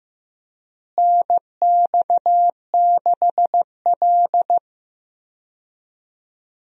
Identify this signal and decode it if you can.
Morse code, heard as a keyed tone, about 16 wpm, 700 Hz. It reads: NX6L
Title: USA Call Signs - 4 Characters - Repeated 2x - 15wpm